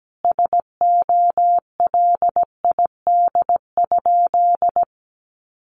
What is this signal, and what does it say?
Morse code, heard as a keyed tone, about 17 wpm, 700 Hz. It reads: SOLID?
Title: Sentences from Top 1500 Words 17wpm